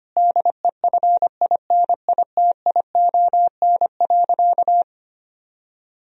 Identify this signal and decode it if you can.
Morse code, heard as a keyed tone, about 25 wpm, 700 Hz. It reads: DEFINITION.
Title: Sentences from Top 1000 Words - Review Part 4 - 25wpm